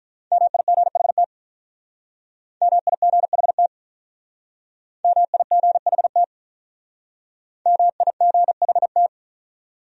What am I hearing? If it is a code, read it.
Morse code, heard as a keyed tone, about 44 wpm, 700 Hz. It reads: MIGHT MIGHT MIGHT MIGHT